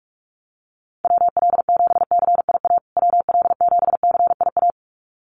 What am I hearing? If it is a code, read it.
Morse code, heard as a keyed tone, about 44 wpm, 700 Hz. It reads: WL7CIA WL7CIA
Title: USA Call Signs Repeated 2x 45wpm